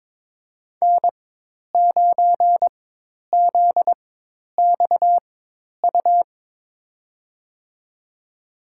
Morse code, heard as a keyed tone, about 22 wpm, spaced as for 10 wpm, 700 Hz. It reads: N9ZXU
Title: USA Call Signs - 5 Characters - 4x Character Spacing - 22wpm